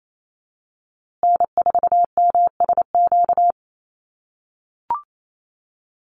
Morse code, heard as a keyed tone, about 28 wpm, 700 Hz. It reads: N4MSQ
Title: USA Call Signs Repeated 2x 28wpm